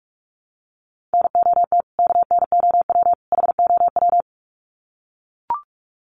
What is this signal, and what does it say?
Morse code, heard as a keyed tone, about 44 wpm, 700 Hz. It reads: NOT KNOW HOW
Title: Sets of 3 Words 45wpm